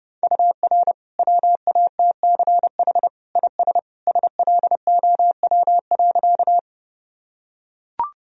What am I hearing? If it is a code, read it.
Morse code, heard as a keyed tone, about 30 wpm, 700 Hz. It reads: UR WATCH IS SLOW.
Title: Sentences from Top 400 Words and Abbreviations 30wpm